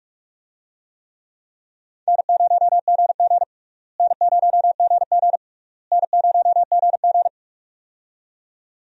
Morse code, heard as a keyed tone, about 46 wpm, 700 Hz. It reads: N0GG N0GG N0GG
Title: USA Call Signs - 4 Characters - Repeated 3x - 45wpm